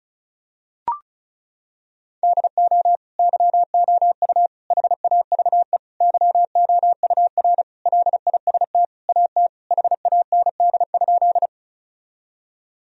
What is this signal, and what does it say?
Morse code, heard as a keyed tone, about 36 wpm, 700 Hz. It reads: DO YOU HAVE YOUR LIST AT HAND?
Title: Sentences from Top 1000 Words - Review Part 2 - 35wpm